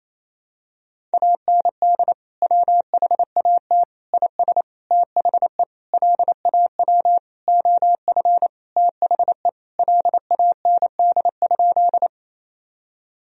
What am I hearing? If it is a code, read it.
Morse code, heard as a keyed tone, about 28 wpm, 700 Hz. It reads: AND WHAT IS THE LAW OF THE LAND?
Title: Sentences from Top 1000 Words - Review Part 3 - 28wpm